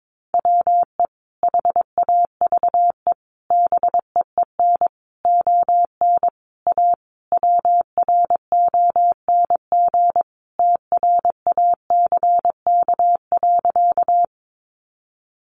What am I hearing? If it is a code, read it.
Morse code, heard as a keyed tone, about 22 wpm, 700 Hz. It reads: WE HAVE BEEN ON A WRONG TRACK.
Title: Sentences from Top 1100 Words 22wpm